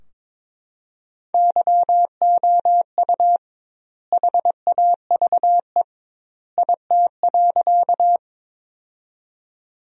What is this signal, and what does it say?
Morse code, heard as a keyed tone, about 22 wpm, 700 Hz. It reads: YOU HAVE IT.
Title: 2x Word-Spacing - Sentences from Top 100 Words - 22wpm